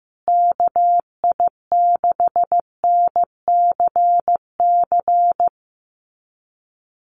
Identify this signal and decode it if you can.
Morse code, heard as a keyed tone, about 15 wpm, 700 Hz. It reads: KI6NCC
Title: USA Call Signs Repeated 3x 15wpm